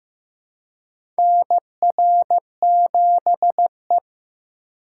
Morse code, heard as a keyed tone, about 15 wpm, 700 Hz. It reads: NR7E
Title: USA Call Signs - 4 Characters - 3x Character Spacing - 15wpm